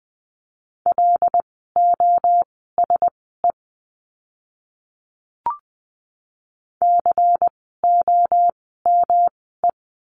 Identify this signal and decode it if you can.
Morse code, heard as a keyed tone, about 20 wpm, spaced as for 15 wpm, 700 Hz. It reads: LOSE COME
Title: 108 Lesson 18f - Letter m - Words 20wpm